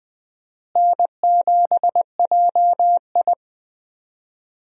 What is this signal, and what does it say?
Morse code, heard as a keyed tone, about 20 wpm, 700 Hz. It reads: N7JI
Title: USA Call Signs - 4 Characters - Repeated 3x - 20wpm